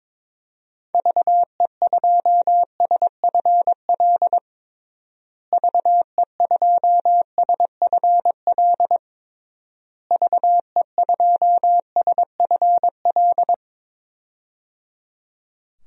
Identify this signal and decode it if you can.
Morse code, heard as a keyed tone, about 22 wpm, 700 Hz. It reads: VE2SFL VE2SFL VE2SFL
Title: DXCC-100 Call Signs - 6 Characters - Repeated 3x - 22wpm